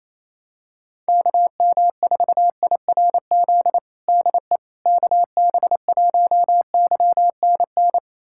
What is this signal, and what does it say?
Morse code, heard as a keyed tone, about 28 wpm, 700 Hz. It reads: KM4IRZ DE KB1YNN
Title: USA Callsign DE Callsign - 6 Characters - 28wpm